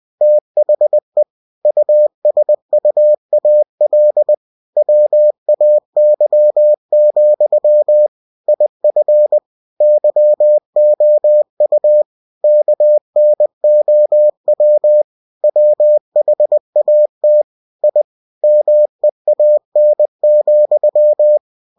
Morse code, heard as a keyed tone, about 20 wpm, 600 Hz. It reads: THE USUAL WAY, IF YOU KNOW WHAT I MEAN,